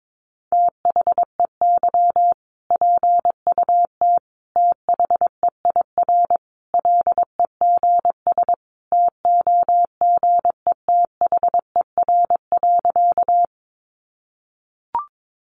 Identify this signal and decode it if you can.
Morse code, heard as a keyed tone, about 22 wpm, 700 Hz. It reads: THEY PUT THEIR LEGS TOGETHER.
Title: Easy Phrases 22wpm